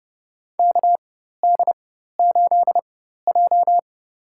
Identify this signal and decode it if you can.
Morse code, heard as a keyed tone, about 32 wpm, spaced as for 14 wpm, 700 Hz. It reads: KD8J
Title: USA Call Signs - 4x Character Spacing - 30wpm